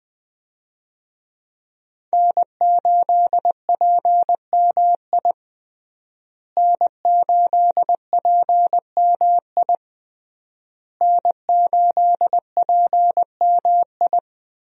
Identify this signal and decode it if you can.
Morse code, heard as a keyed tone, about 20 wpm, 700 Hz. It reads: N8PMI N8PMI N8PMI
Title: USA Call Signs - 5 Characters - Repeated 3x - 20wpm